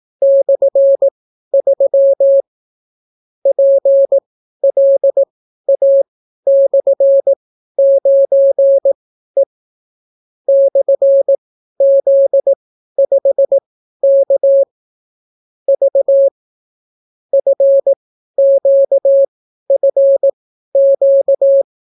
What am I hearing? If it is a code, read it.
Morse code, heard as a keyed tone, about 18 wpm, spaced as for 12 wpm, 550 Hz. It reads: /3 PLA/9E /Z5K V FQFQ